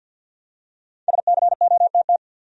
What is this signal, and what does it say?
Morse code, heard as a keyed tone, about 53 wpm, 700 Hz. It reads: SCOTT
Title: USA Common Names - Max 5 Letters - 50wpm